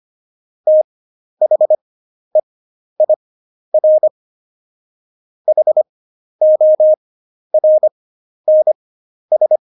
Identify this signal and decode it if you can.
Morse code, heard as a keyed tone, about 25 wpm, spaced as for 11 wpm, 620 Hz. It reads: THEIR HORNS
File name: Jokes-25-char-10-eff